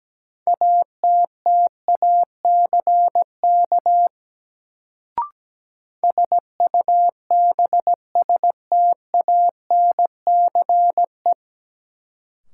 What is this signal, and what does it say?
Morse code, heard as a keyed tone, about 17 wpm, 700 Hz. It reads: ATTACK SUBSTANCE
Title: Sentences from Top 1500 Words - New Word Drill - 17wpm